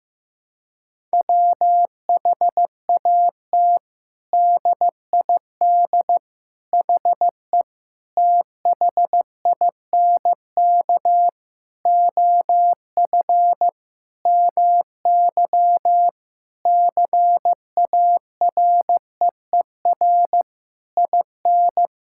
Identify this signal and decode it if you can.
Morse code, heard as a keyed tone, about 15 wpm, 700 Hz. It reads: WHAT DID HE THINK OF MY CAREER IN
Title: Sentences from Top 1000 Words - Review Part 6 - 15wpm